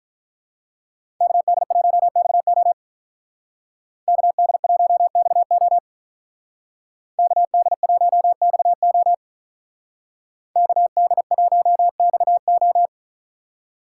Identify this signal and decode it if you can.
Morse code, heard as a keyed tone, about 45 wpm, 700 Hz. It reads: KD1XO KD1XO KD1XO KD1XO